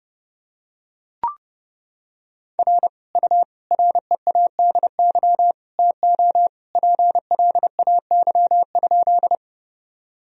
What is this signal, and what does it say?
Morse code, heard as a keyed tone, about 30 wpm, 700 Hz. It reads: R U READY TO PLAY?
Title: Sentences from Top 400 Words and Abbreviations 30wpm